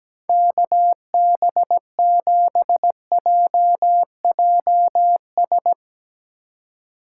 Morse code, heard as a keyed tone, about 17 wpm, 700 Hz. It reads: KB7JJS
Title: Speed Racing - USA 2x3 Call Signs 17wpm